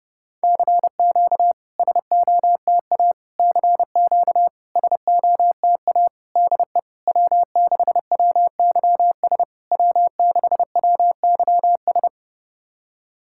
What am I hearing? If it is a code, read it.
Morse code, heard as a keyed tone, about 30 wpm, 700 Hz. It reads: CQ SOTA CQ SOTA DE W6WYS W6WYS
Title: SOTA QSO Practice 30wpm